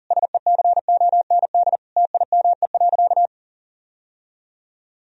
Morse code, heard as a keyed tone, about 37 wpm, 700 Hz. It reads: SECOND TIME.
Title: Sentences from Top 1000 Words - Review Part 1 - 40wpm